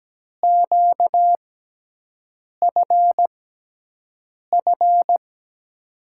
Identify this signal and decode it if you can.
Morse code, heard as a keyed tone, about 17 wpm, spaced as for 6 wpm, 700 Hz. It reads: QFF
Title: USA Call Signs - 6x Character Spacing - 17wpm